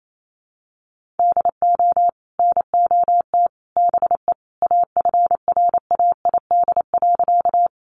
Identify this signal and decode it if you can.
Morse code, heard as a keyed tone, about 28 wpm, 700 Hz. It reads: DO NOT BE AFRAID.
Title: Sentences from Top 1500 Words 28wpm